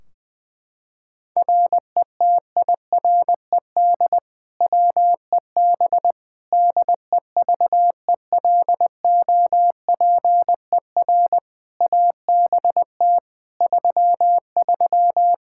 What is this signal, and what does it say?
Morse code, heard as a keyed tone, about 20 wpm, 700 Hz. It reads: RETIRED WEB DEVELOPER ABT 33
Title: K1USN SST QSO Practice 20wpm